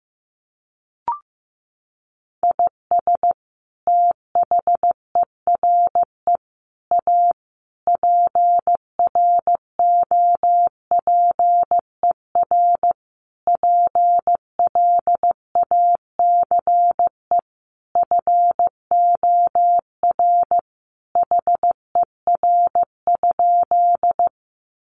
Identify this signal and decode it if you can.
Morse code, heard as a keyed tone, about 15 wpm, 700 Hz. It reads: IS THERE A PROPER PLACE FOR HER?
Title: Sentences from Top 800 Words 15wpm